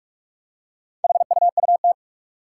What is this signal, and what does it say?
Morse code, heard as a keyed tone, about 44 wpm, 700 Hz. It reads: HUUT
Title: Four Challenging Letter-Number Combinations 45wpm